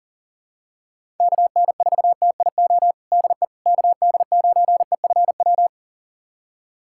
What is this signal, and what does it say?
Morse code, heard as a keyed tone, about 37 wpm, 700 Hz. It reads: KN4TIO DE KD9EFW
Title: USA Callsign DE Callsign - 6 Characters - 40wpm